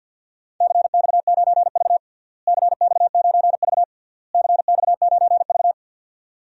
Morse code, heard as a keyed tone, about 53 wpm, 700 Hz. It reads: CX9V CX9V CX9V
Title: DXCC-100 Call Signs - 4 Characters - Repeated 3x - 50wpm